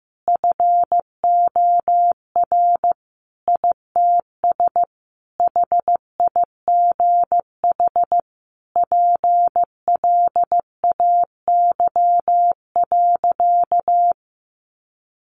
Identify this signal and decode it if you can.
Morse code, heard as a keyed tone, about 15 wpm, 700 Hz. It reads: FOR ITS HIGH PLAY.
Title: Sentences from Top 1500 Words 15wpm